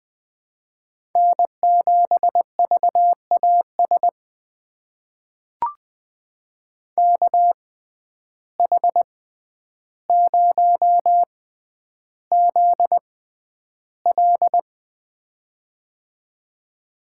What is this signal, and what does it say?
Morse code, heard as a keyed tone, about 20 wpm, spaced as for 10 wpm, 700 Hz. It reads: N7VAS KH0ZL
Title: USA Call Signs - 5 Characters - 6x Character Spacing - 20wpm